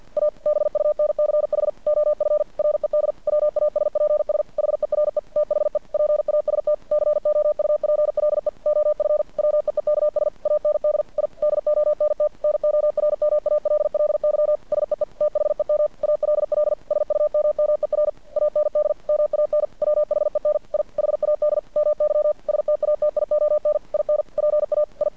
Morse code, heard as a keyed tone, about 50 wpm, 600 Hz. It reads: A BUNCH OF RED WASPS HERE THE PAST COUPLE OF WEEKS AND I DONT NORMALLY SEE THEM ALL SUMMER AND MAN WHEN I HAD MY STATION IN PA I
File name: archived / CRSnet